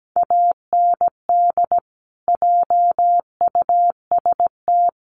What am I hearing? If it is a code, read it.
Morse code, heard as a keyed tone, about 17 wpm, 700 Hz. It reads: AND JUST
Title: Sentences from Top 1400 Words 17wpm